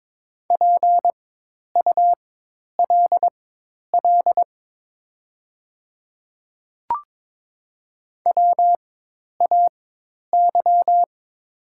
Morse code, heard as a keyed tone, about 22 wpm, spaced as for 10 wpm, 700 Hz. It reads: PULL WAY
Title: 4x Character Spacing - Top 500 Words - 22wpm